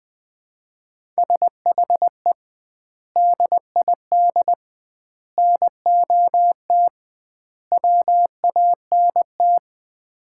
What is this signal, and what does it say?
Morse code, heard as a keyed tone, about 20 wpm, 700 Hz. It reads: SHE DID NOT WANT
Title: 2x Word-Spacing - Sets of 4 Words - 20wpm